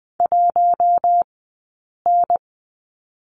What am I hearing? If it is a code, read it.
Morse code, heard as a keyed tone, about 20 wpm, 700 Hz. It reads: 1 N
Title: 040 Lesson 8d - Number 1 - Two Characters 20wpm